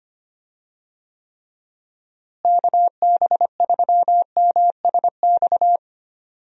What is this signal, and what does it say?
Morse code, heard as a keyed tone, about 25 wpm, 700 Hz. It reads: KB3MSX
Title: USA Call Signs - 6 Characters - Repeated 3x - 25wpm